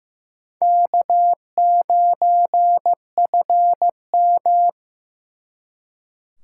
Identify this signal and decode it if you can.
Morse code, heard as a keyed tone, about 18 wpm, 700 Hz. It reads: K9FM
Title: Speed Racing - USA 1x2 Call Signs 15wpm